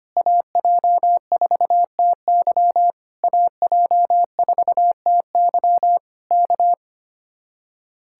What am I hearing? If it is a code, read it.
Morse code, heard as a keyed tone, about 24 wpm, 700 Hz. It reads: AJ4TY AJ4TY K
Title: Speed-Racing Common QSO Phrases - 25wpm